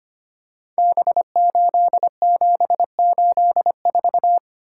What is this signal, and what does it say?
Morse code, heard as a keyed tone, about 25 wpm, 700 Hz. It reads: B8784
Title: Code Groups - 5 Letter:Numbers - 25wpm